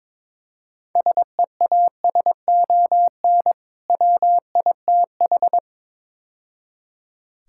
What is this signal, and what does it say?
Morse code, heard as a keyed tone, about 22 wpm, 700 Hz. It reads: SEASON WITH
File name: practice / 22wpm / Build A Phrase - 5-Word Phrases - 22wpm